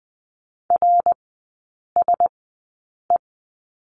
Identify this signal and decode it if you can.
Morse code, heard as a keyed tone, about 20 wpm, spaced as for 8 wpm, 700 Hz. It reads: RSE
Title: 081 Lesson 14e - Number 2 - Three Characters 20wpm